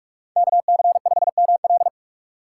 Morse code, heard as a keyed tone, about 40 wpm, 700 Hz. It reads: KC5ML